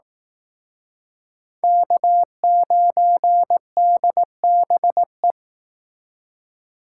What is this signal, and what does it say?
Morse code, heard as a keyed tone, about 17 wpm, 700 Hz. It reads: K9DBE
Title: Speed Racing - USA 1x3 Call Signs 15wpm